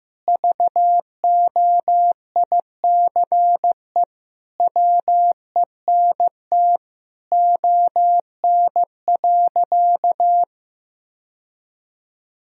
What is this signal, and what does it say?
Morse code, heard as a keyed tone, about 16 wpm, 700 Hz. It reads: VOICE WENT ON.
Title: Sentences from Top 1000 Words - Review Part 1 - 15wpm